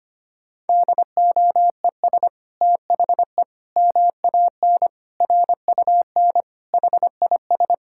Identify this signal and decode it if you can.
Morse code, heard as a keyed tone, about 25 wpm, 700 Hz. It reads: DOES THE MAN RUN HIS
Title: Sentences from Top 1000 Words - Review Part 5 - 25wpm